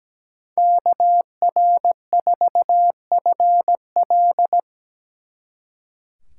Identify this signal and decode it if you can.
Morse code, heard as a keyed tone, about 20 wpm, 700 Hz. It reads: KR4FL
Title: Speed Racing - USA 2x2 Call Signs 17wpm